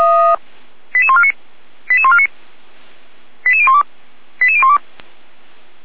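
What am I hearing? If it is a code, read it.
Five-tone selective call (ZVEI), heard as quick runs of five tones, tones of about 70 ms. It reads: 88149 88149 88012 88012
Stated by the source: ZVEI five-tone selective call received on an SDR, from the feuerwehr-orlamuende server (fire brigade Orlamünde).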